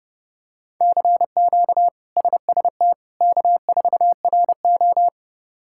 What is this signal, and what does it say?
Morse code, heard as a keyed tone, about 32 wpm, 700 Hz. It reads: CQ SST K4RO
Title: K1USN SST QSO Practice - 30wpm